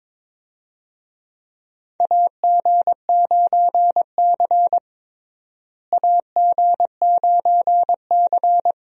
Morse code, heard as a keyed tone, about 22 wpm, 700 Hz. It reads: AG9C AG9C
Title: USA Call Signs - 4 Characters - Repeated 2x - 22wpm